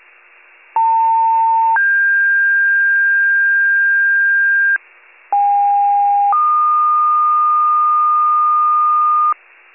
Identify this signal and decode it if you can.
Two-tone page, heard as a one-second tone, then a three-second tone, about 900 then 1600 Hz, about 810 then 1220 Hz, with noise. It is heard from 0.8 to 9.3 s.